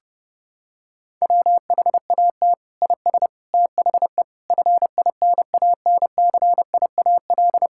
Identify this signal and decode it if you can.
Morse code, heard as a keyed tone, about 30 wpm, 700 Hz. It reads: WHAT IS THE FINANCIAL
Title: Sentences from Top 1000 Words - Review Part 5 - 30wpm